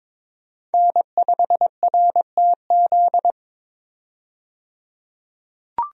Morse code, heard as a keyed tone, about 22 wpm, 700 Hz. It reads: N5RTZ
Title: USA Call Signs - 5 Characters - 22wpm